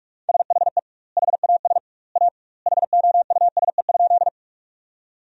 Morse code, heard as a keyed tone, about 46 wpm, 700 Hz. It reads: SHE HAS A HOUSE?
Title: 2x Word-Spacing - Sentences from Top 200 Words - 45wpm